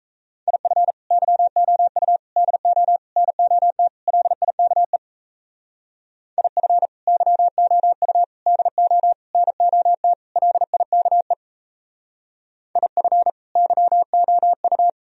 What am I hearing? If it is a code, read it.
Morse code, heard as a keyed tone, about 35 wpm, 700 Hz. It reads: IF YOU DO NOT LIKE IF YOU DO NOT LIKE IF YOU
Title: Speed-Racing - Sets of 5 Words 28wpm